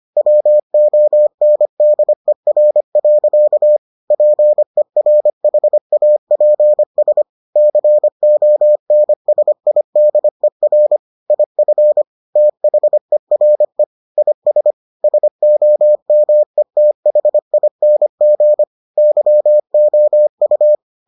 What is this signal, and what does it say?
Morse code, heard as a keyed tone, about 25 wpm, 600 Hz. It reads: WONDER. PERHAPS CONSIDER IF THERE IS SOMETHING YOU